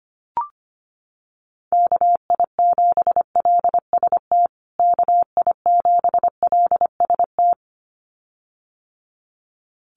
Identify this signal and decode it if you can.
Morse code, heard as a keyed tone, about 25 wpm, 700 Hz. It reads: KI7LST KI7LST
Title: USA Call Signs Repeated 2x 25wpm